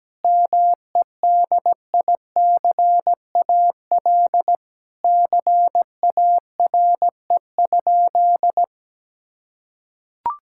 Morse code, heard as a keyed tone, about 17 wpm, 700 Hz. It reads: MEDICAL CARE?
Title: Sentences from Top 1000 Words - Review Part 5 - 17wpm